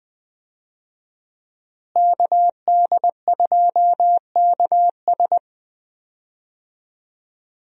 Morse code, heard as a keyed tone, about 20 wpm, 700 Hz. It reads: KD2KS
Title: USA Call Signs - 5 Characters - Repeated 3x - 20wpm